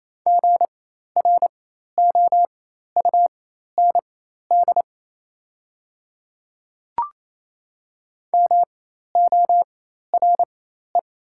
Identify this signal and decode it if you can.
Morse code, heard as a keyed tone, about 28 wpm, spaced as for 13 wpm, 700 Hz. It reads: GROUND MORE
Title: 4x Character Spacing - Top 500 Words - 28wpm